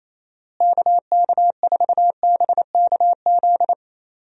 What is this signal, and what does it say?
Morse code, heard as a keyed tone, about 28 wpm, 700 Hz. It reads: KK4BKZ
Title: USA Call Signs - 6 Characters - Repeated 3x - 28wpm